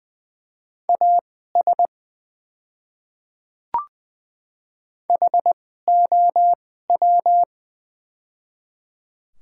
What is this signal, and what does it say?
Morse code, heard as a keyed tone, about 20 wpm, spaced as for 15 wpm, 700 Hz. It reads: AS HOW
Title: 2x Character Spacing - Top 100 Words - 20wpm